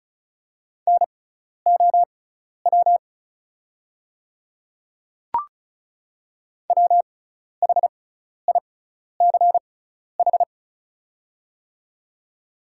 Morse code, heard as a keyed tone, about 36 wpm, spaced as for 12 wpm, 700 Hz. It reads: NOW WHICH
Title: 6x Character Spacing - Top 100 Words - 35wpm